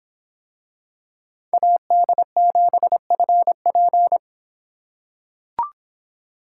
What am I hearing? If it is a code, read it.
Morse code, heard as a keyed tone, about 23 wpm, 700 Hz. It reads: AD7FP